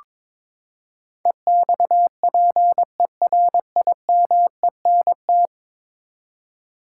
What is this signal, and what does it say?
Morse code, heard as a keyed tone, about 22 wpm, 700 Hz. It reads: EXPERIMENT
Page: morse-code-ninja-practice-22wpm